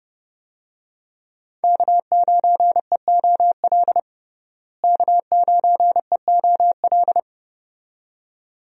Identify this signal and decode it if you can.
Morse code, heard as a keyed tone, about 30 wpm, 700 Hz. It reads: K9EOL K9EOL
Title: USA Call Signs - 5 Characters - Repeated 2x - 30wpm